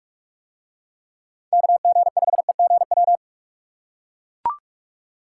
Morse code, heard as a keyed tone, about 44 wpm, 700 Hz. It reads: KG5EGW